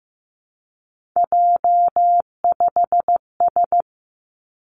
Morse code, heard as a keyed tone, about 15 wpm, 700 Hz. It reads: J5S
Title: Three Challenging Letter-Number Combinations 15wpm